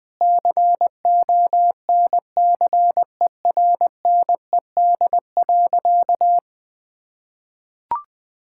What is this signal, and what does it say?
Morse code, heard as a keyed tone, about 20 wpm, 700 Hz. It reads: CONCERNED.
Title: Sentences from Top 1000 Words - Review Part 6 - 20wpm